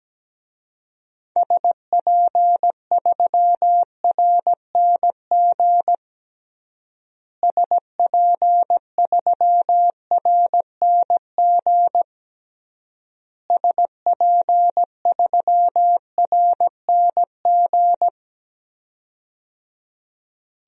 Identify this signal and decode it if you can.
Morse code, heard as a keyed tone, about 17 wpm, 700 Hz. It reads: SP3RNG SP3RNG SP3RNG
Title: DXCC-100 Call Signs - 6 Characters - Repeated 3x - 17wpm